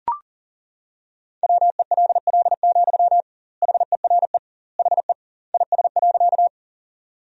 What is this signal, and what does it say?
Morse code, heard as a keyed tone, about 40 wpm, 700 Hz. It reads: WELL, HERE HE IS.
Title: 2x Word-Spacing - Sentences from Top 200 Words - 40wpm